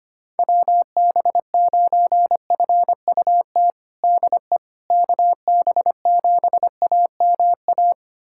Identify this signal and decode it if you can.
Morse code, heard as a keyed tone, about 25 wpm, 700 Hz. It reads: WB9FUT DE KB7AMA
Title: USA Callsign DE Callsign - 6 Characters - 25wpm